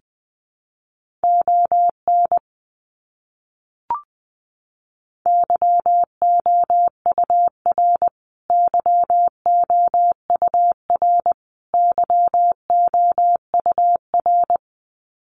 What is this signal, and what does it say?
Morse code, heard as a keyed tone, about 20 wpm, 700 Hz. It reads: ON YOUR YOUR YOUR
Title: Top 200 Words Repeated 20wpm